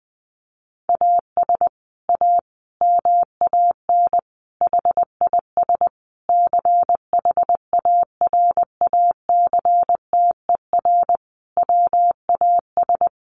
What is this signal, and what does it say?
Morse code, heard as a keyed tone, about 20 wpm, 700 Hz. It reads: AS A MAN HIS CHARACTER WAS
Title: Sentences from Top 1200 Words 20wpm